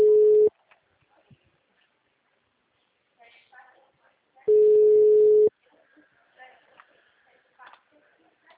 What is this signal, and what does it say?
Ringback tone (Europe, 425 Hz), with speech.